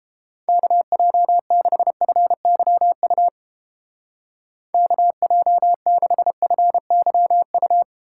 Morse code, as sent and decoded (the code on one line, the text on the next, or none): -.- .--- -.... ..-. -.-- ..- / -.- .--- -.... ..-. -.-- ..-
KJ6FYU KJ6FYU